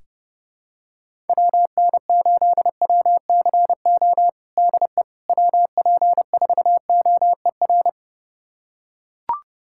Morse code, heard as a keyed tone, about 32 wpm, 700 Hz. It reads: WN8WCO DE WP4OER
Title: USA Callsign DE Callsign - 6 Characters - 30wpm